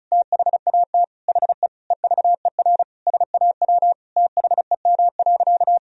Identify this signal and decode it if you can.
Morse code, heard as a keyed tone, about 36 wpm, 700 Hz. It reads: THAT HE EVER SAW THEM.
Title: Sentences from Top 1000 Words - Review Part 2 - 35wpm